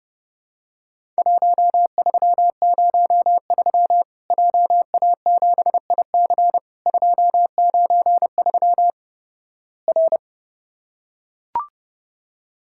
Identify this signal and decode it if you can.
Morse code, heard as a keyed tone, about 30 wpm, 700 Hz. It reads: 1303 JA7IC 293 R
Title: Worked All Europe DX Contest - QTC Exchange - 30wpm